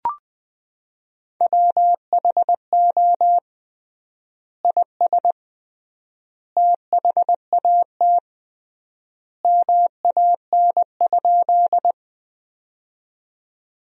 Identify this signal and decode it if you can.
Morse code, heard as a keyed tone, about 21 wpm, 700 Hz. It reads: WHO IS THAT MAN?
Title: 3x Word-Spacing - Sentences from Top 200 Words - 20wpm